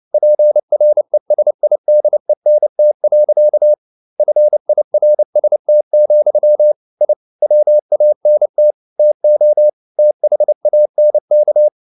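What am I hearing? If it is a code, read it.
Morse code, heard as a keyed tone, about 29 wpm, 600 Hz. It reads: PRESIDENT. FIRST, I WANT TO THANK
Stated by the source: Morse code practice at 28 wpm, text of Ken Follett's novel Never in English, from CW med Gnister.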